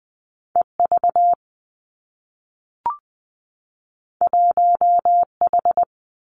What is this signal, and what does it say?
Morse code, heard as a keyed tone, about 20 wpm, 700 Hz. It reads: E V 1 H
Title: Two Challenging Letter-Number Combinations 20wpm